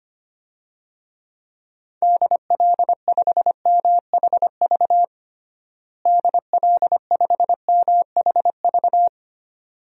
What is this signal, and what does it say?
Morse code, heard as a keyed tone, about 25 wpm, 700 Hz. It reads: DL5MHV DL5MHV
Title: DXCC-100 Call Signs - 6 Characters - Repeated 2x - 25wpm